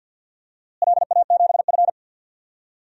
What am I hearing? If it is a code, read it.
Morse code, heard as a keyed tone, about 50 wpm, 700 Hz. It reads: LA7F
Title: DXCC-100 Call Signs - 4 Characters - Repeated 2x - 50wpm